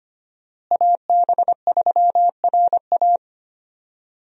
Morse code, heard as a keyed tone, about 25 wpm, 700 Hz. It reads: AB3RA
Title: USA Call Signs - 5 Characters - Repeated 3x - 25wpm